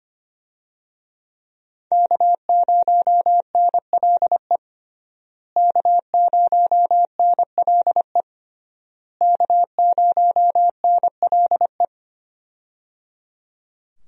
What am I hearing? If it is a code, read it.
Morse code, heard as a keyed tone, about 25 wpm, 700 Hz. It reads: K0NLE K0NLE K0NLE
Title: USA Call Signs - 5 Characters - Repeated 3x - 25wpm